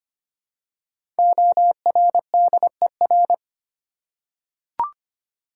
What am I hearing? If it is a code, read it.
Morse code, heard as a keyed tone, about 25 wpm, 700 Hz. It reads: ORDER
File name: 5 Letter Words - First 3 Letters Given - 25wpm